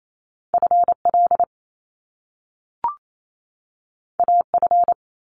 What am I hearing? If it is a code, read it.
Morse code, heard as a keyed tone, about 28 wpm, 700 Hz. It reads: FL AF
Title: Top Two Letter Combinations in Words 28wpm